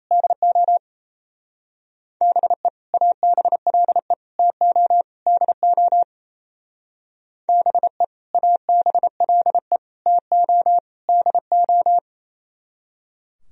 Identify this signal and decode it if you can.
Morse code, heard as a keyed tone, about 36 wpm, 700 Hz. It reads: DO BE ABLE TO DO BE ABLE TO DO